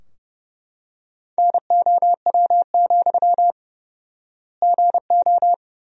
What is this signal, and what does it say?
Morse code, heard as a keyed tone, about 32 wpm, 700 Hz. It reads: NOW, GO
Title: 4x Word-Spacing - Sentences from Top 200 Words - 30wpm